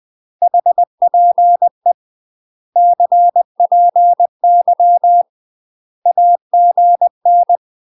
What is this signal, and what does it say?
Morse code, heard as a keyed tone, about 20 wpm, 700 Hz. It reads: HPE CPY AGN
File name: UK MASTER QSOs TRAINING / 20WPM / UK MASTER QSOs 0068